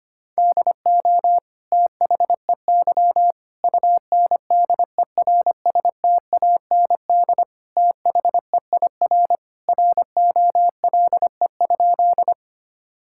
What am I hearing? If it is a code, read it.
Morse code, heard as a keyed tone, about 25 wpm, 700 Hz. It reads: DO THEY UNDERSTAND THEIR ROLE?
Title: Sentences from Top 1000 Words - Review Part 6 - 25wpm